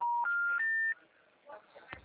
Special information tone.
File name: out- -2004- - - .39512